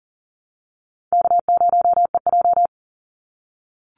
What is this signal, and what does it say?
Morse code, heard as a keyed tone, about 40 wpm, 700 Hz. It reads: K0EJ
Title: Call Signs - 40wpm